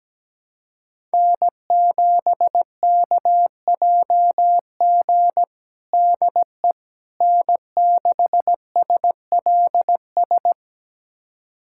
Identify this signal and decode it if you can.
Morse code, heard as a keyed tone, about 17 wpm, 700 Hz. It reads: N7KJG DE N6SLS
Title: USA Callsign DE Callsign - 5 Characters - 17wpm